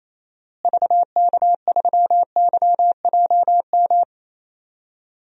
Morse code, heard as a keyed tone, about 28 wpm, 700 Hz. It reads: VK3YJM
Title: DXCC-100 Call Signs - 6 Characters - Repeated 2x - 28wpm